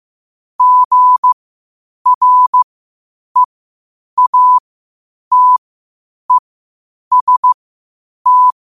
Morse code, heard as a keyed tone, about 15 wpm, spaced as for 8 wpm, 1000 Hz. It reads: GREATEST
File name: qotdfiles / 8wpm